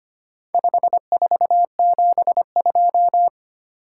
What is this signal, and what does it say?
Morse code, heard as a keyed tone, about 25 wpm, 700 Hz. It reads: 5472
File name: practice / 25wpm / Four Letter-Number Combinations - 25wpm